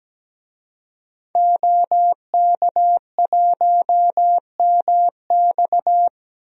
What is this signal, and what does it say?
Morse code, heard as a keyed tone, about 17 wpm, 700 Hz. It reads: OK1MX
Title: DXCC-100 Call Signs - 5 Characters - 17wpm